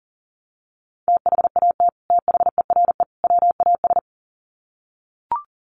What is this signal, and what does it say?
Morse code, heard as a keyed tone, about 37 wpm, 700 Hz. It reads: THAT THERE WAS
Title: Sets of 3 Words 40wpm